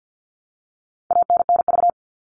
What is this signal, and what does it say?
Morse code, heard as a keyed tone, about 48 wpm, 700 Hz. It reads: ANNV
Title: All Cut Numbers - 4-Digit 50wpm